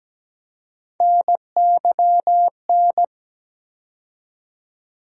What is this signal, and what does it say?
Morse code, heard as a keyed tone, about 17 wpm, 700 Hz. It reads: NYN
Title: USA Call Sign Postfixes - 3 Letters 17wpm